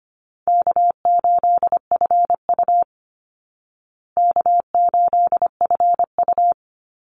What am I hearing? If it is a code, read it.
Morse code, heard as a keyed tone, about 25 wpm, 700 Hz. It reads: K8FU K8FU
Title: Easy Call Signs Repeated 3x - 25wpm